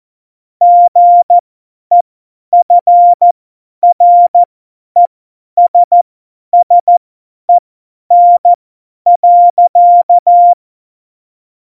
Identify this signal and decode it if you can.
Morse code, heard as a keyed tone, about 14 wpm, spaced as for 10 wpm, 700 Hz. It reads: GEFRESSEN.
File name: Morsekurs / Witze / Witz 01. August 17, Tempo 50 F70